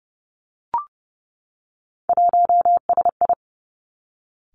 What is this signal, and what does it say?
Morse code, heard as a keyed tone, about 30 wpm, 700 Hz. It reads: 1SI
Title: Three Challenging Letter-Number Combinations 30wpm